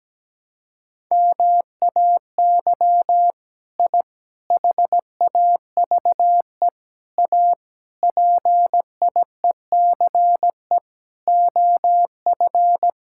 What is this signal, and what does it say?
Morse code, heard as a keyed tone, about 17 wpm, 700 Hz. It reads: MAY I HAVE A PIECE OF
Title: Sentences from Top 1000 Words - Review Part 5 - 17wpm